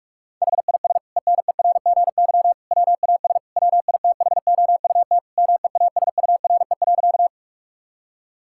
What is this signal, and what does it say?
Morse code, heard as a keyed tone, about 43 wpm, 690 Hz. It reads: HIS ENERGY WAS WITHOUT MEASURE.
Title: Sentences from Top 1000 Words - Review Part 6 - 45wpm